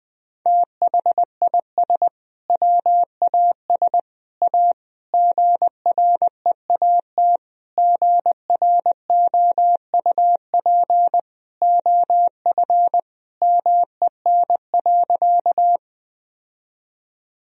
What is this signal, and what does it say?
Morse code, heard as a keyed tone, about 20 wpm, 700 Hz. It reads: THIS WAS A GREAT GROUP OF MEN.
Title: Sentences from Top 1000 Words - Review Part 1 - 20wpm